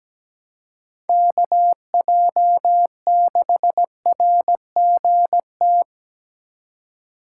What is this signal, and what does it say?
Morse code, heard as a keyed tone, about 20 wpm, 700 Hz. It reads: KJ6RGT
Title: Speed Racing - USA 2x3 Call Signs 17wpm